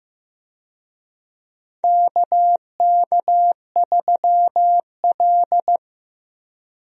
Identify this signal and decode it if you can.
Morse code, heard as a keyed tone, about 15 wpm, 700 Hz. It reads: KK3L
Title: USA Call Signs - 4 Characters - 15wpm